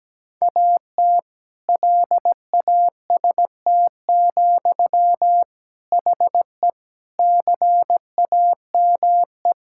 Morse code, heard as a keyed tone, about 17 wpm, 700 Hz. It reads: AT LAST, HE CAME
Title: Sentences from Top 1600 Words 17wpm